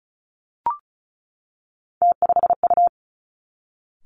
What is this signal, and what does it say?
Morse code, heard as a keyed tone, about 36 wpm, 700 Hz. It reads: T5U
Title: Three Challenging Letter-Number Combinations 35wpm